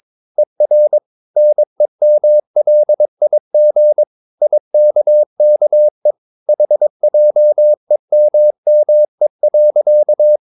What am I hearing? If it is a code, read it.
Morse code, heard as a keyed tone, about 22 wpm, 600 Hz. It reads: ER NEMLIG IKKE HJEMME.
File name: Ferie QRS 22wpm